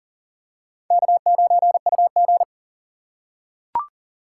Morse code, heard as a keyed tone, about 37 wpm, 700 Hz. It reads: K9UG
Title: USA Call Signs - 4 Characters - 3x Character Spacing - 40wpm